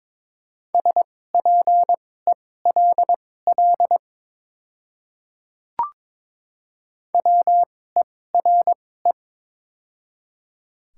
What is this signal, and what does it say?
Morse code, heard as a keyed tone, about 22 wpm, spaced as for 16 wpm, 700 Hz. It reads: SPELL WERE
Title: 2x Character Spacing - Top 200 Words - 22wpm